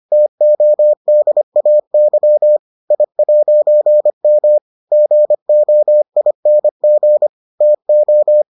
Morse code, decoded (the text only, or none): TODAY I'M GOING TO